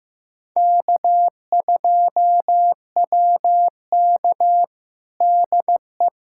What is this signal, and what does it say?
Morse code, heard as a keyed tone, about 15 wpm, 700 Hz. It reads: K2WK DE
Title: USA Callsign DE Callsign - 4 Characters - 15wpm